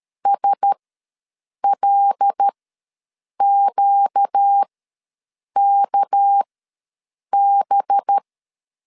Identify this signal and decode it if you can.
Morse code, heard as a keyed tone, about 13 wpm, spaced as for 7 wpm, 800 Hz. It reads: SLQKB
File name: LEZIONI / 6 PKX0 / 6 PKX0 ws 12wpm 3448